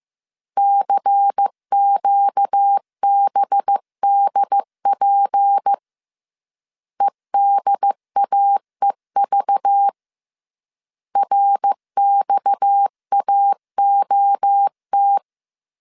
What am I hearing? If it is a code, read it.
Morse code, heard as a keyed tone, about 15 wpm, 800 Hz. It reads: CQBDP EDAEV RXAOT
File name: CW / DET / 1 6 1500C S C 14wpm 2951